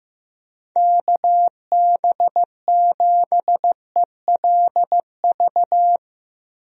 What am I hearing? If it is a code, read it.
Morse code, heard as a keyed tone, about 16 wpm, 700 Hz. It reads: KB7ELV
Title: USA Call Signs - 6 Characters - 15wpm